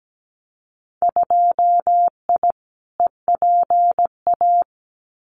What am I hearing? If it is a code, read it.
Morse code, heard as a keyed tone, about 17 wpm, 700 Hz. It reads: 2I EPA
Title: Winter Field Day Exchanges 17wpm